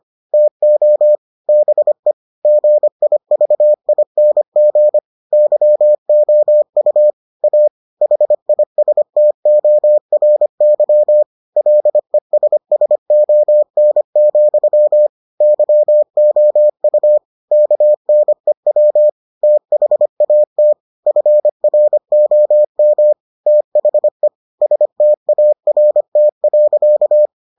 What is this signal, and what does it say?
Morse code, heard as a keyed tone, about 25 wpm, 600 Hz. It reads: TO BE GIVING YOU A HISTORY LESSON, YOU KNEW THAT FROM THE START.